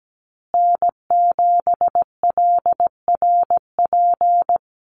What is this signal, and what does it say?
Morse code, heard as a keyed tone, about 17 wpm, 700 Hz. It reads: N7LRP
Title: USA Call Signs Repeated 2x 17wpm